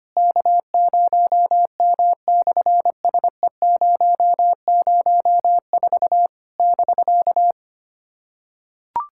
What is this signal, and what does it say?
Morse code, heard as a keyed tone, about 24 wpm, 700 Hz. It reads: K0M/SE004 <BK>
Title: SOTA QSO Practice 25wpm